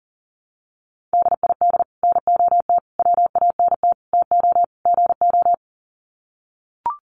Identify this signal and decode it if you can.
Morse code, heard as a keyed tone, about 40 wpm, 700 Hz. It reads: DID NOT WANT TO GO
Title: Sets of 5 Words 40wpm